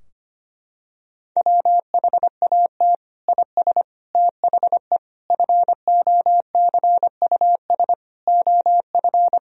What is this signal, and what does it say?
Morse code, heard as a keyed tone, about 25 wpm, 700 Hz. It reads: WHAT IS THE FOCUS OF
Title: Sentences from Top 1000 Words - Review Part 4 - 25wpm